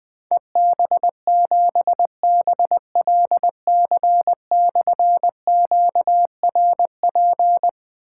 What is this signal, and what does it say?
Morse code, heard as a keyed tone, about 21 wpm, 700 Hz. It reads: EB7BLC/QRP
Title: Difficult Call Signs 20wpm